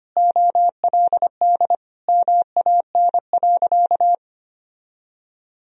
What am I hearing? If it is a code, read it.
Morse code, heard as a keyed tone, about 25 wpm, 700 Hz. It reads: OLD MAN.
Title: Sentences from Top 200 Words and Abbreviations 25wpm